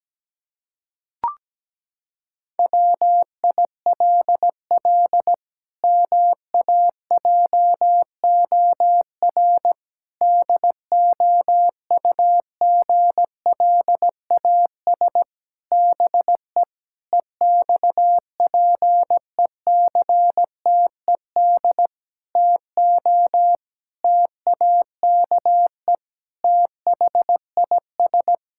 Morse code, heard as a keyed tone, about 17 wpm, 700 Hz. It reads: WILL MAJOR DOUGLAS BE EXPECTED TO TAKE THIS
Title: Pangram Challenge v2 - 17wpm